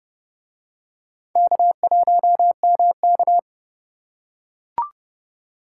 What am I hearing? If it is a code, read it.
Morse code, heard as a keyed tone, about 29 wpm, 700 Hz. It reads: K1MK